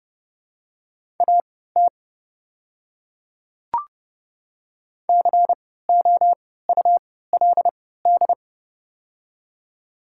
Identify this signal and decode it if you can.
Morse code, heard as a keyed tone, about 30 wpm, spaced as for 17 wpm, 700 Hz. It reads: AT COULD
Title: 3x Character Spacing - Top 100 Words - 30wpm